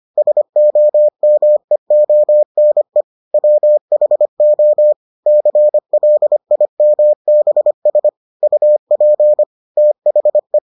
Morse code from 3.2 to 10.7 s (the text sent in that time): WHO CLIMBS UP THE